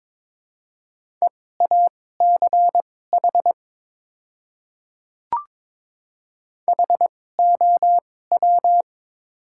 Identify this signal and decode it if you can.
Morse code, heard as a keyed tone, about 22 wpm, spaced as for 16 wpm, 700 Hz. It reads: EACH HOW
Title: 2x Character Spacing - Top 200 Words - 22wpm